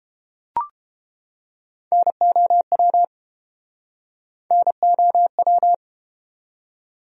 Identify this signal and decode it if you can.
Morse code, heard as a keyed tone, about 30 wpm, 700 Hz. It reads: NOW NOW